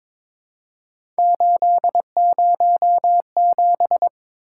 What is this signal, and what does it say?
Morse code, heard as a keyed tone, about 22 wpm, 700 Hz. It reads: 807
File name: practice / 22wpm / Three Letter-Number Combinations - 22wpm